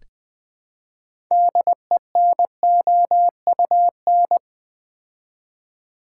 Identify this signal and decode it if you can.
Morse code, heard as a keyed tone, about 20 wpm, 700 Hz. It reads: DENOUN
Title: Build A Word - 3 Letters At A Time - 8-Letter Words - 20wpm